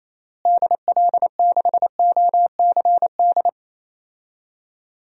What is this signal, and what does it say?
Morse code, heard as a keyed tone, about 28 wpm, 700 Hz. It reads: DL6OCD
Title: DXCC-100 Call Signs - 6 Characters - Repeated 3x - 28wpm